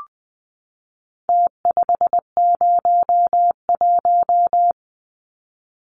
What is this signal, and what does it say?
Morse code, heard as a keyed tone, about 20 wpm, 700 Hz. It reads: T501
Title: Numbers - Contest Style - 4-Digit 20wpm